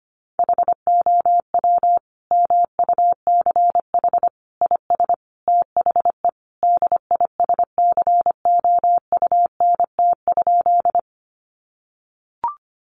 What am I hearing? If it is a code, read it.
Morse code, heard as a keyed tone, about 25 wpm, 700 Hz. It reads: HOW MUCH IS THE DISCOUNT?
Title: Sentences from Top 1200 Words 25wpm